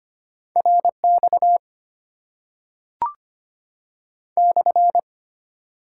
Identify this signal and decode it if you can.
Morse code, heard as a keyed tone, about 25 wpm, 700 Hz. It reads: R X /
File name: Top 90 QSO Elements - 8x Character Spacing - 25wpm